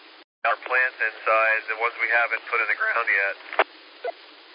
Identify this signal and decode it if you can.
Emergency Alert System (SAME) tones, heard as harsh buzzing bursts, with speech. It is heard from 0.8 to 3.5 s.